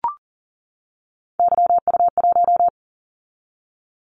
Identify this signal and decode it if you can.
Morse code, heard as a keyed tone, about 40 wpm, 700 Hz. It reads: YU1
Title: Callsign Prefixes for Top 25 DXCC - 40wpm